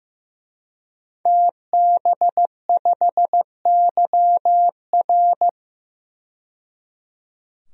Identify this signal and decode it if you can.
Morse code, heard as a keyed tone, about 15 wpm, 700 Hz. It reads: TB5YR
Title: DXCC-100 Call Signs - 5 Characters - 15wpm